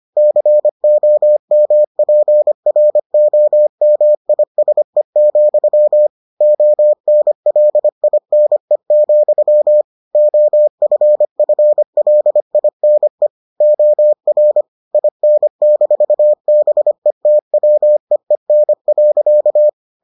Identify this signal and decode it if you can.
Morse code, heard as a keyed tone, about 25 wpm, 600 Hz. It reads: COMPROMISE, ONLINE, OFFLINE OR IN-BETWEEN.